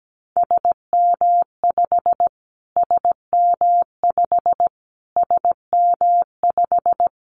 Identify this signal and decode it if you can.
Morse code, heard as a keyed tone, about 17 wpm, 700 Hz. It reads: SM5 SM5 SM5
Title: Callsign Prefixes for Top 25 DXCC - Repeated 3x - 17wpm